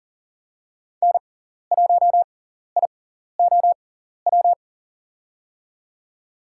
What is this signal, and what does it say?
Morse code, heard as a keyed tone, about 40 wpm, spaced as for 14 wpm, 700 Hz. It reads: N1IOW
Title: USA Call Signs - 6x Character Spacing - 40wpm